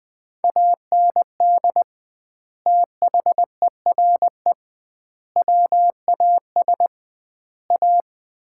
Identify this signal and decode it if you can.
Morse code, heard as a keyed tone, about 21 wpm, 700 Hz. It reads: AND THERE WAS A
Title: 2x Word-Spacing - Sets of 4 Words - 20wpm